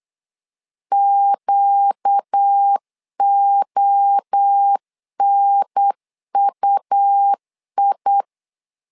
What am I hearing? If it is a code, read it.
Morse code, heard as a keyed tone, about 8 wpm, 800 Hz. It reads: QONUI
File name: 4 Q 8wpm